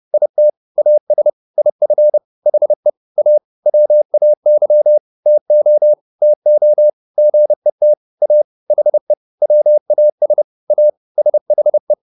Morse code, heard as a keyed tone, about 30 wpm, 600 Hz. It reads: IT AS IF HE A WAY TO TO GET A HE WAS A SHE
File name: WD201 30